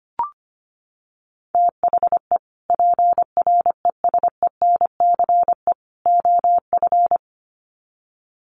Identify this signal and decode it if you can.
Morse code, heard as a keyed tone, about 25 wpm, 700 Hz. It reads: THE PRESENCE OF
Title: Sets of 3 Words 25wpm